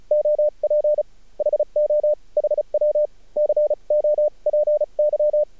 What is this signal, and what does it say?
Morse code, heard as a keyed tone, about 34 wpm, spaced as for 30 wpm, 600 Hz. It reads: OP HO HW COPY
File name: archived / CRSnet